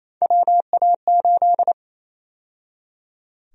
Morse code, heard as a keyed tone, about 28 wpm, 700 Hz. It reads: WA8
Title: USA Call Sign Prefixes - 28wpm